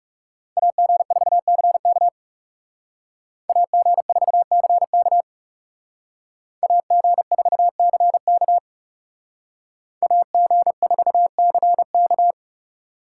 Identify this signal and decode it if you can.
Morse code, heard as a keyed tone, about 38 wpm, 700 Hz. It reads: AG4CK AG4CK AG4CK AG4CK